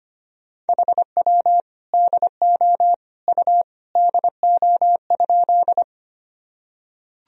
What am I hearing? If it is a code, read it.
Morse code, heard as a keyed tone, about 25 wpm, 700 Hz. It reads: HW DO U DO?
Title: Sentences from Top 100 Words and Abbreviations 25wpm